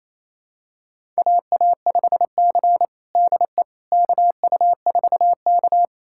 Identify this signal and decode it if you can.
Morse code, heard as a keyed tone, about 28 wpm, 700 Hz. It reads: AA5C DE KU4K
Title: USA Callsign DE Callsign - 4 Characters - 28wpm